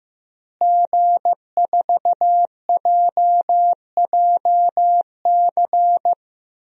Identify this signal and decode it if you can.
Morse code, heard as a keyed tone, about 16 wpm, 700 Hz. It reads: G4JJC
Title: DXCC-100 Call Signs - 5 Characters - Repeated 2x - 15wpm